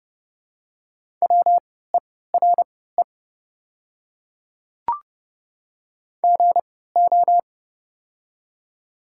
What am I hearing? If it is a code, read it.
Morse code, heard as a keyed tone, about 30 wpm, spaced as for 17 wpm, 700 Hz. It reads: WERE GO
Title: 3x Character Spacing - Top 100 Words - 30wpm